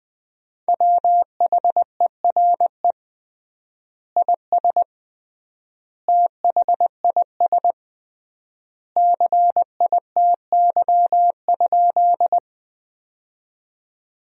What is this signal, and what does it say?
Morse code, heard as a keyed tone, about 21 wpm, 700 Hz. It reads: WHERE IS THIS CITY?
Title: 3x Word-Spacing - Sentences from Top 300 Words - 20wpm